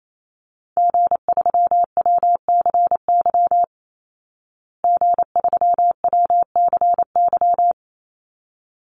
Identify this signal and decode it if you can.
Morse code, heard as a keyed tone, about 28 wpm, 700 Hz. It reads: G3WCY G3WCY
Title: Call Signs Repeated 3x - 28wpm